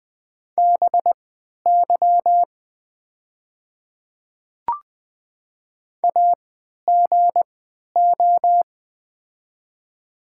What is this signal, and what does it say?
Morse code, heard as a keyed tone, about 20 wpm, spaced as for 11 wpm, 700 Hz. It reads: BY AGO
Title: 3x Character Spacing - Top 500 Words - 20wpm